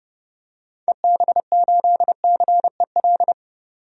Morse code, heard as a keyed tone, about 30 wpm, 700 Hz. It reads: EB8CEL
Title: DXCC-100 Call Signs - 6 Characters - 30wpm